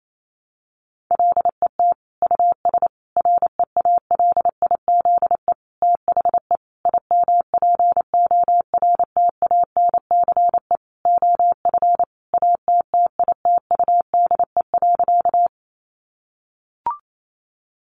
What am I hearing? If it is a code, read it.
Morse code, heard as a keyed tone, about 28 wpm, 700 Hz. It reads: LET US REALIZE THE IMPORTANCE OF ATTITUDE.
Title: Sentences from Top 1300 Words 28wpm